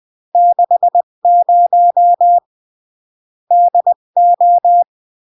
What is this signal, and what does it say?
Morse code, heard as a keyed tone, about 20 wpm, 700 Hz. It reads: 60 DO
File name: CW / jodorowsky / 82 commandments 20-15